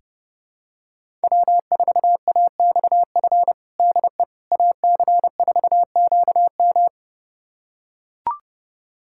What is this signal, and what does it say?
Morse code, heard as a keyed tone, about 30 wpm, 700 Hz. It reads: W4AXF DE AC4QM
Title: USA Callsign DE Callsign - 5 Characters - 30wpm